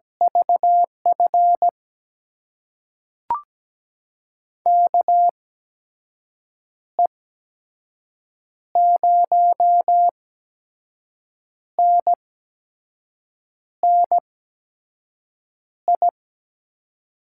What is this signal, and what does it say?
Morse code, heard as a keyed tone, about 17 wpm, spaced as for 5 wpm, 700 Hz. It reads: VF KE0NNI
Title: USA Call Signs - 6 Characters - 8x Character Spacing - 17wpm